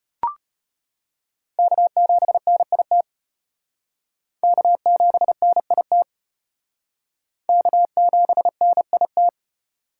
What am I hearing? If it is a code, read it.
Morse code, heard as a keyed tone, about 31 wpm, 700 Hz. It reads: K7NIT K7NIT K7NIT
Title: Speed Racing - USA 1x3 Call Signs 25wpm